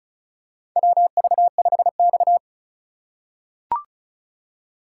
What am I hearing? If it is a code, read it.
Morse code, heard as a keyed tone, about 36 wpm, 700 Hz. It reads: WV5X
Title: USA Call Signs - 4 Characters - 4x Character Spacing - 35wpm